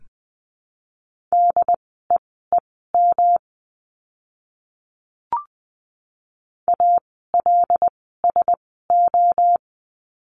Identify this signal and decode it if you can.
Morse code, heard as a keyed tone, about 20 wpm, spaced as for 14 wpm, 700 Hz. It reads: DEEM ALSO
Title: 108 Lesson 18f - Letter m - Words 20wpm